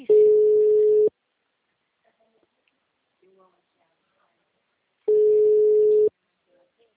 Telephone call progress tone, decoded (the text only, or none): ringback tone